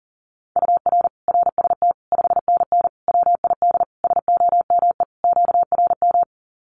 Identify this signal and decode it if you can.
Morse code, heard as a keyed tone, about 40 wpm, 700 Hz. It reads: UR RST 5NN WID SOME QRM
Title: Common QSO Phrases 40wpm